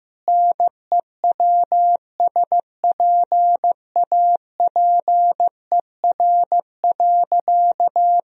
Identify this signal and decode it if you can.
Morse code, heard as a keyed tone, about 15 wpm, 700 Hz. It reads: NEWSPAPER.
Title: Sentences from Top 1000 Words - Review Part 4 - 15wpm